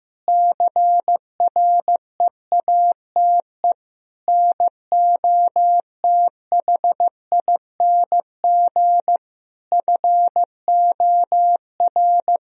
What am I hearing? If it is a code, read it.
Morse code, heard as a keyed tone, about 15 wpm, 700 Hz. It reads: CREATE NOTHING FOR
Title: Sentences from Top 1000 Words - Review Part 3 - 15wpm